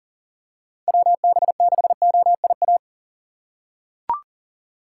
Morse code, heard as a keyed tone, about 40 wpm, 700 Hz. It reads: WB6OIA
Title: USA Call Signs - 6 Characters - 2x Character Spacing - 40wpm